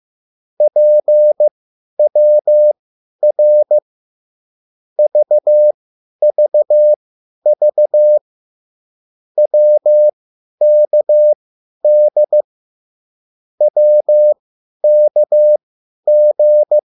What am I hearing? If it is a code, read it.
Morse code, heard as a keyed tone, about 15 wpm, spaced as for 10 wpm, 600 Hz. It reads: PWR VVV WKD WKG